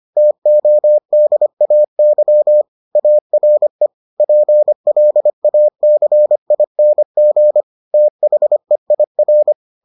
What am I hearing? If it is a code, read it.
Morse code, heard as a keyed tone, about 25 wpm, 600 Hz. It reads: TODAY ARE PLACING THEIR